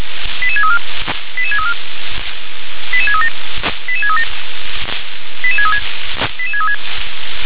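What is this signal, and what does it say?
Five-tone selective call (ZVEI), heard as quick runs of five tones, tones of about 70 ms. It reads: 88634 88634 88637 88637 88636 88636
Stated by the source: ZVEI five-tone call 88747 received by SDR.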